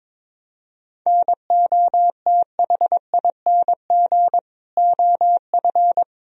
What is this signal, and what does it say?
Morse code, heard as a keyed tone, about 22 wpm, 700 Hz. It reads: NOTHING OF
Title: Sentences from Top 1000 Words - Review Part 3 - 22wpm